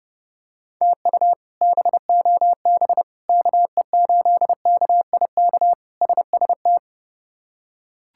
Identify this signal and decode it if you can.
Morse code, heard as a keyed tone, about 30 wpm, 700 Hz. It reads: TU BOB KE8KIK SST
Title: K1USN SST QSO Practice - 30wpm